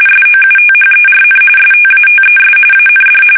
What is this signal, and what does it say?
RTTY, about 45.45 baud. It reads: SOUTHERN STATES AND
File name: fsk850 / usnavy1